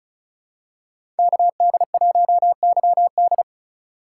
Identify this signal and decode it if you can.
Morse code, heard as a keyed tone, about 36 wpm, 700 Hz. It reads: KD1YD
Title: USA Call Signs - 5 Characters - 3x Character Spacing - 35wpm